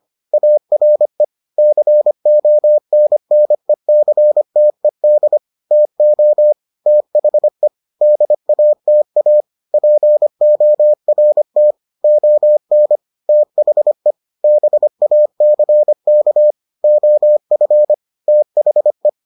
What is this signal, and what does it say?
Morse code, heard as a keyed tone, about 25 wpm, 600 Hz. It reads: ARE CONNECTED TO THE DATA PORT ON THE BACK OF THE